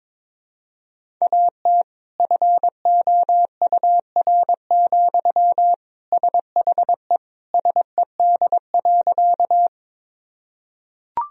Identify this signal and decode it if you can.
Morse code, heard as a keyed tone, about 21 wpm, 700 Hz. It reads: AT FOUR, SHE SED.
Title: Sentences from Top 300 Words and Abbreviations 22wpm